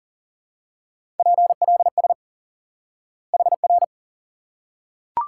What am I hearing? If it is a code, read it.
Morse code, heard as a keyed tone, about 40 wpm, 700 Hz. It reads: PLS HR
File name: IQE Course 3 - Lesson 89 - YAGI - Two Elements - 40wpm